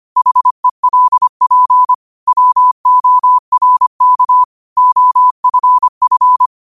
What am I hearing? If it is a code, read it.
Morse code, heard as a keyed tone, about 25 wpm, 1000 Hz. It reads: HELP WORK OFF